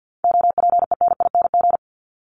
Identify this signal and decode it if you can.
Morse code, heard as a keyed tone, about 50 wpm, 700 Hz. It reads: OPENING
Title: Top 1751-2000 Words 50wpm